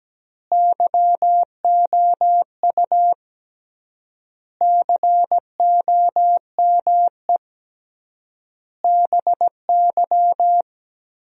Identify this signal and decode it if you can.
Morse code, heard as a keyed tone, about 17 wpm, 700 Hz. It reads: YOU COME BY
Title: 3x Word-Spacing - Sentences from Top 100 Words - 17wpm